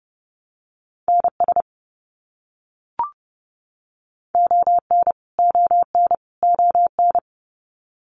Morse code, heard as a keyed tone, about 30 wpm, 700 Hz. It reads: NS ON ON ON
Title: Canadian Provinces and Territories - Repeated 3x - 30wpm